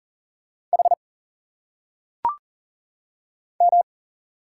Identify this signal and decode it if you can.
Morse code, heard as a keyed tone, about 39 wpm, 700 Hz. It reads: H M
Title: Single Letters - Phonetic Pronunciation 40wpm